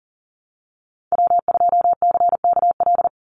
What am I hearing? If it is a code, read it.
Morse code, heard as a keyed tone, about 40 wpm, 700 Hz. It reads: W2CKL
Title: NA SOTA Activator Callsigns 40wpm